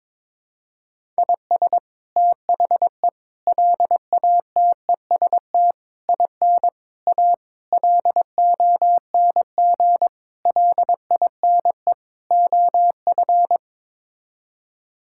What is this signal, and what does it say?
Morse code, heard as a keyed tone, about 21 wpm, 700 Hz. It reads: IS THE LATEST IN A LONG LINE OF
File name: practice / 22wpm / Sets of 8 Words - 22wpm